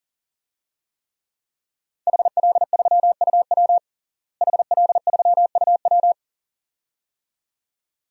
Morse code, heard as a keyed tone, about 40 wpm, 700 Hz. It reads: HL3UW HL3UW
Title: DXCC-100 Call Signs - 5 Characters - Repeated 2x - 40wpm